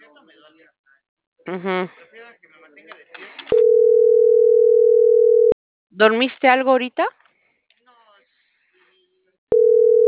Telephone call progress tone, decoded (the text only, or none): ringback tone